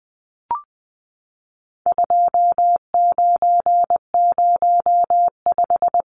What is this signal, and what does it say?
Morse code, heard as a keyed tone, about 20 wpm, 700 Hz. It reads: 2905
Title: Numbers - 4-Digit 20wpm